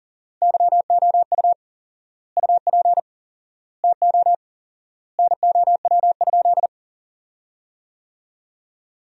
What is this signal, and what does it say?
Morse code, heard as a keyed tone, about 40 wpm, 700 Hz. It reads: YOU UP TO NOW?
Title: 4x Word-Spacing - Sentences from Top 100 Words - 40wpm